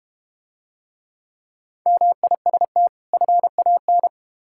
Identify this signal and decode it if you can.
Morse code, heard as a keyed tone, about 32 wpm, 700 Hz. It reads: MIST FAN